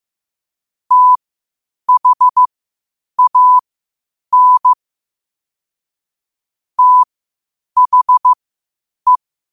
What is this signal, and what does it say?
Morse code, heard as a keyed tone, about 15 wpm, spaced as for 8 wpm, 1000 Hz. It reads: THAN THE
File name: qotdfiles / 8wpm